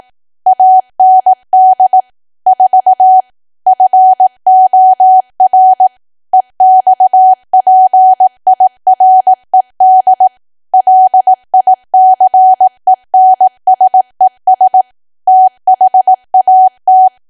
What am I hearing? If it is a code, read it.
Morse code, heard as a keyed tone, about 18 wpm, 750 Hz. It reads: AND 4 FOR EXPIRED LICENSES THAT
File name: cwa / W1AW-Level2 / 18WPM